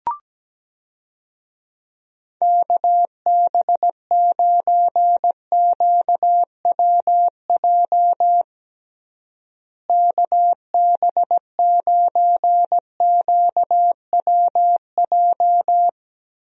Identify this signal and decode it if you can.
Morse code, heard as a keyed tone, about 17 wpm, 700 Hz. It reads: KB9QWJ KB9QWJ
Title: USA Call Signs - 6 Characters - Repeated 2x - 17wpm